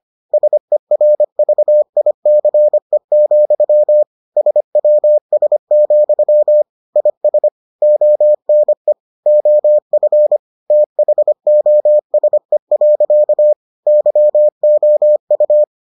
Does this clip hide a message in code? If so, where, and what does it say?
Yes: Morse code from 0.3 to 15.6 s, heard as a keyed tone, about 25 wpm, 600 Hz. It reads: SERVICE, SWS, IS ONE OF THOSE. YOU